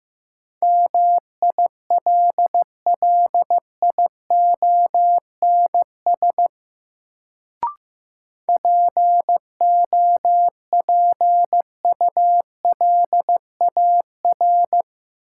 Morse code, heard as a keyed tone, about 15 wpm, 700 Hz. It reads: MILLIONS POPULAR
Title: Sets of 5 Words - New Word Drill - 15wpm